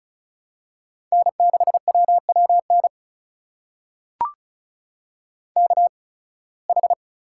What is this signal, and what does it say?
Morse code, heard as a keyed tone, about 34 wpm, spaced as for 23 wpm, 700 Hz. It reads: N6WWN KH
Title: USA Call Signs - 5 Characters - 8x Character Spacing - 35wpm